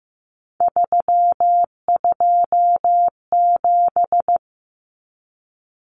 Morse code, heard as a keyed tone, about 15 wpm, 700 Hz. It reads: 327
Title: Common Cut Numbers - 3-Digit 15wpm